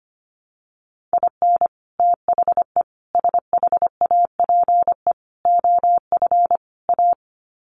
Morse code, heard as a keyed tone, about 25 wpm, 700 Hz. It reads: IN THE SHAPE OF A
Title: Sets of 5 Words Encore 25wpm